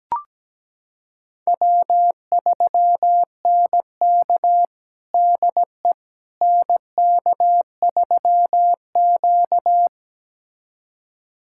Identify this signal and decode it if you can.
Morse code, heard as a keyed tone, about 17 wpm, 700 Hz. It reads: W3NK DE NK3Q
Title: USA Callsign DE Callsign - 4 Characters - 17wpm